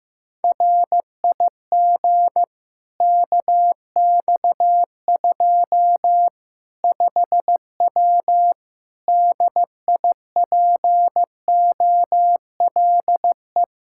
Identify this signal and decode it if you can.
Morse code, heard as a keyed tone, about 15 wpm, 700 Hz. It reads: RIG KX2 5W DIPOLE
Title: Practice QSO - Exchange 2 - Info - 15wpm